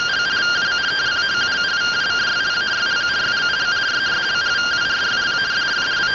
Two-tone teleprinter signal, about 170 Hz shift.
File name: findings / CT 1 Scout 0002.gen-42 rtty rx